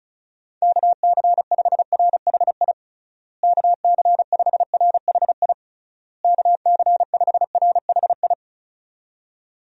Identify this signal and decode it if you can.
Morse code, heard as a keyed tone, about 36 wpm, 700 Hz. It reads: KC5RHI KC5RHI KC5RHI
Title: USA Call Signs - 6 Characters - Repeated 3x - 35wpm